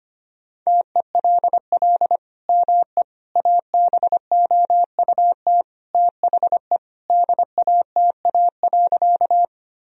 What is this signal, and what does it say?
Morse code, heard as a keyed tone, about 25 wpm, 700 Hz. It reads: TELL ME ABOUT THE DATA.
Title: Sentences from Top 1000 Words - Review Part 3 - 25wpm